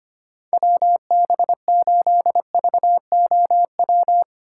Morse code, heard as a keyed tone, about 25 wpm, 700 Hz. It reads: WB8VOW
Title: USA Call Signs - 6 Characters - 2x Character Spacing - 25wpm